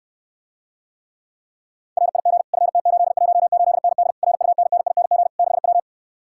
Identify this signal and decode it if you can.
Morse code, heard as a keyed tone, about 69 wpm, 700 Hz. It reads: RIG FT818ND RUNNING 6W